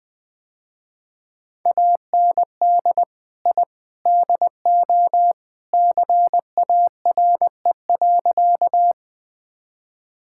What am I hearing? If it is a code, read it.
Morse code, heard as a keyed tone, about 21 wpm, 700 Hz. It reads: AND I DO CARE.
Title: Sentences with 3-Word Phrases - Spoken Beginning - 20wpm